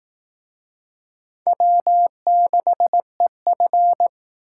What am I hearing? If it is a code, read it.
Morse code, heard as a keyed tone, about 18 wpm, 700 Hz. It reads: W6EF
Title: Speed Racing - USA 1x2 Call Signs 15wpm